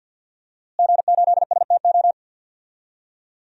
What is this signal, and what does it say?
Morse code, heard as a keyed tone, about 53 wpm, 700 Hz. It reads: G8STO
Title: DXCC-100 Call Signs - 5 Characters - Repeated 3x - 50wpm